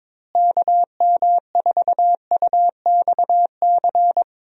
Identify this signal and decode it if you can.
Morse code, heard as a keyed tone, about 22 wpm, 700 Hz. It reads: KM4UXC
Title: USA Call Signs - 6 Characters - Repeated 2x - 22wpm